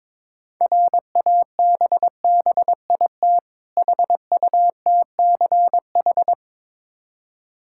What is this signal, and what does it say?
Morse code, heard as a keyed tone, about 22 wpm, 690 Hz. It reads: RABBIT HUTCH
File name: Difficult Words with Hints - 22wpm